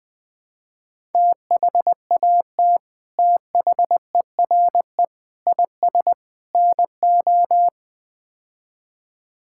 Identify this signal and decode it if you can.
Morse code, heard as a keyed tone, about 21 wpm, 700 Hz. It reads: THAT THERE IS NO
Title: Build A Phrase - 4-Word Phrases - 20wpm